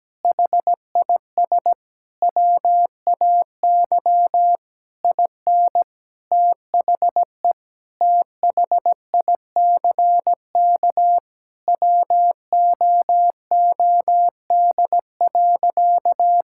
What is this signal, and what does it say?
Morse code, heard as a keyed tone, about 17 wpm, 700 Hz. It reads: HIS WAY IN THE THICK WOOD.
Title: Sentences from Top 1600 Words 17wpm